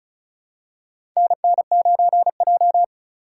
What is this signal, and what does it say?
Morse code, heard as a keyed tone, about 36 wpm, 700 Hz. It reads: NN9J
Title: USA Call Signs - 4 Characters - 3x Character Spacing - 35wpm